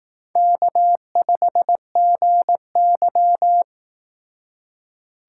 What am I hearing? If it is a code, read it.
Morse code, heard as a keyed tone, about 18 wpm, 700 Hz. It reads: K5GY
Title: Speed Racing - USA 1x2 Call Signs 15wpm